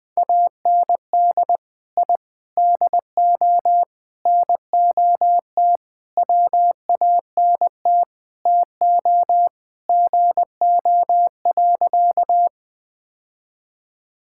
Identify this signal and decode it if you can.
Morse code, heard as a keyed tone, about 19 wpm, 700 Hz. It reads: AND I DO NOT WANT TO GO.
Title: Sentences with 5-Word Phrases - 20wpm